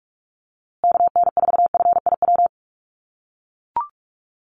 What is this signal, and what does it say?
Morse code, heard as a keyed tone, about 44 wpm, 700 Hz. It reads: KN4FIW
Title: USA Call Signs Repeated 2x 45wpm